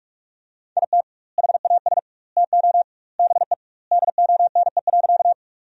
Morse code, heard as a keyed tone, about 46 wpm, 700 Hz. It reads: IT HAS TO BE DONE.
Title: 2x Word-Spacing - Sentences from Top 500 Words - 45wpm